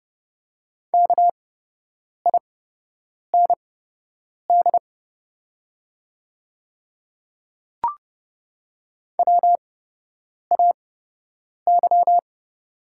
Morse code, heard as a keyed tone, about 30 wpm, spaced as for 8 wpm, 700 Hz. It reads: KIND WAY
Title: 8x Character Spacing - Top 200 Words - 30wpm